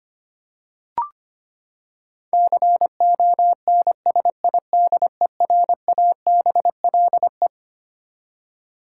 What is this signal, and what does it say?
Morse code, heard as a keyed tone, about 25 wpm, 700 Hz. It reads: CONSIDERABLE
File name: Twelve Letter Words - 25wpm